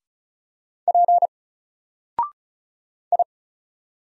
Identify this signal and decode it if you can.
Morse code, heard as a keyed tone, about 35 wpm, 700 Hz. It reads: P I
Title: Single Letter-Number Alt 35wpm